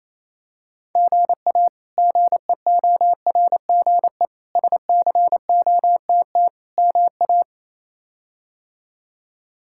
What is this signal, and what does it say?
Morse code, heard as a keyed tone, about 27 wpm, 700 Hz. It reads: GA GEORGE SCOTT MA
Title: K1USN SST QSO Practice - 28wpm